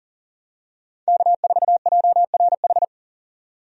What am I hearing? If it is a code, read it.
Morse code, heard as a keyed tone, about 38 wpm, 700 Hz. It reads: K4JRH
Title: USA Call Signs - 5 Characters - 40wpm